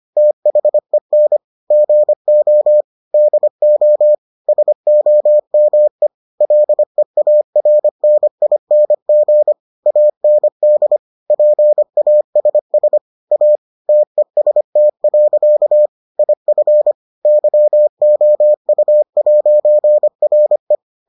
Morse code, heard as a keyed tone, about 25 wpm, 600 Hz. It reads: THEN GO DO SOME LEARNING AND PASS A TEST. IF YOU'RE